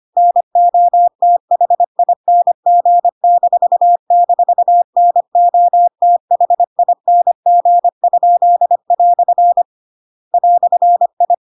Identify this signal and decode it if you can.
Morse code, heard as a keyed tone, about 25 wpm, 700 Hz. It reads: NOTHING--NOTHING?" "I
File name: Pinocchio / Pinocchio ch7 25 25 ik1hge